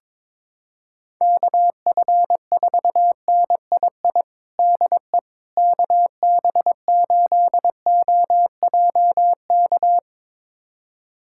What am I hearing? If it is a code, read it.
Morse code, heard as a keyed tone, about 21 wpm, 700 Hz. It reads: KF4NII DE KB8OJK
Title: USA Callsign DE Callsign - 6 Characters - 22wpm